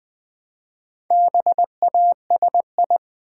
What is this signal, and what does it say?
Morse code, heard as a keyed tone, about 20 wpm, 700 Hz. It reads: BASI
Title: Build A Word - 1 Letter At A Time - 5-Letter Words - 20wpm